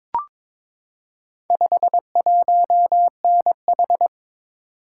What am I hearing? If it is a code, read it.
Morse code, heard as a keyed tone, about 22 wpm, 700 Hz. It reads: 51NH
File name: practice / 22wpm / Four Letter-Number Combinations - 22wpm